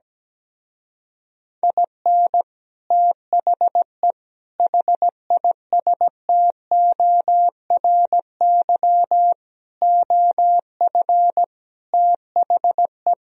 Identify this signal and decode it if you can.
Morse code, heard as a keyed tone, about 17 wpm, 700 Hz. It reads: IN THE HISTORY OF THE
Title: Speed-Racing - Sets of 5 Words 17wpm